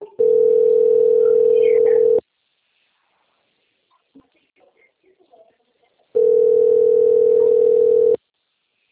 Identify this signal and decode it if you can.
Ringback tone (North America), 440 + 480 Hz.